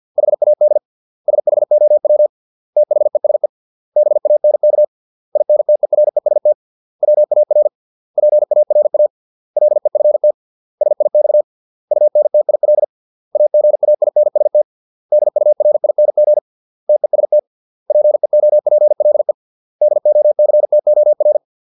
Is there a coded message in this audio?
Yes: Morse code from 0.2 to 21.4 s, heard as a keyed tone, about 50 wpm, 600 Hz. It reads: HAD SHOW THESE BANK INTEREST WAR PARA LEFT SIX UNTIL AGAINST DURING TEST PEOPLE DOCTOR